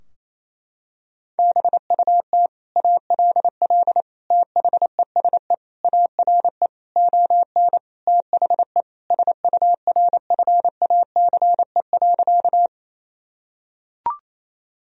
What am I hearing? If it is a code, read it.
Morse code, heard as a keyed tone, about 28 wpm, 700 Hz. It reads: BUT ALL THESE ARE ON THE SURFACE.
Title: Sentences from Top 1000 Words - Review Part 3 - 28wpm